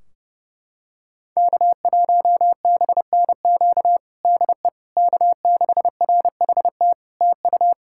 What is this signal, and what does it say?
Morse code, heard as a keyed tone, about 32 wpm, 700 Hz. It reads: K1BNQ DE K6RHT TU
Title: K1USN SST QSO Practice 30wpm